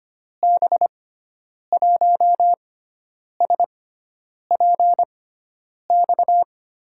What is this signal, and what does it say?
Morse code, heard as a keyed tone, about 25 wpm, spaced as for 9 wpm, 700 Hz. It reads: B1SPX
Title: USA Call Signs - 6 Characters - 6x Character Spacing - 25wpm